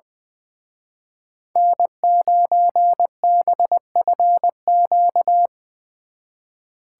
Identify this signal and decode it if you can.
Morse code, heard as a keyed tone, about 22 wpm, 700 Hz. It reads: N9BFQ
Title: Speed Racing - USA 1x3 Call Signs 20wpm